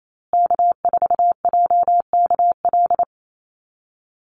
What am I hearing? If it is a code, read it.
Morse code, heard as a keyed tone, about 28 wpm, 700 Hz. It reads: K4JKL
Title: Call Signs Repeated 3x - 28wpm